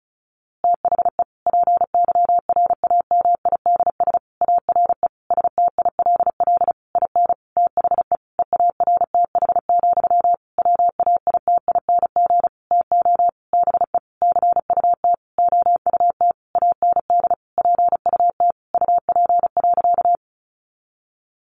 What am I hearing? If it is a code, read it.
Morse code, heard as a keyed tone, about 34 wpm, 700 Hz. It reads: THE PYRAMIDS ARE STILL IN THE EARTH, WAITING TO BE CUT OUT AND PUT UP.
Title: A Sound Of Thunder 35wpm